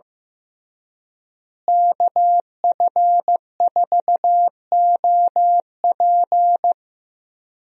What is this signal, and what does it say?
Morse code, heard as a keyed tone, about 15 wpm, 700 Hz. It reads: KF4OP
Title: USA Call Signs - 5 Characters - Repeated 3x - 15wpm